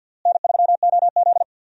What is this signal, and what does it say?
Morse code, heard as a keyed tone, about 50 wpm, 690 Hz. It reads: N3OZ
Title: USA Call Signs - 4 Characters - Repeated 2x - 50wpm